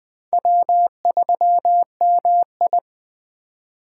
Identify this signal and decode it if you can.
Morse code, heard as a keyed tone, about 20 wpm, 700 Hz. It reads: W3MI
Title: USA Call Signs - 4 Characters - 2x Character Spacing - 20wpm